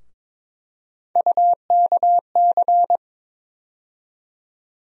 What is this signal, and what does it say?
Morse code, heard as a keyed tone, about 22 wpm, 700 Hz. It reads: UKC
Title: USA Call Sign Postfixes - 3 Letters 22wpm